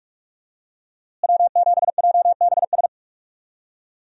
Morse code, heard as a keyed tone, about 42 wpm, 700 Hz. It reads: W7JBS
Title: USA Call Signs - 5 Characters - Repeated 2x - 45wpm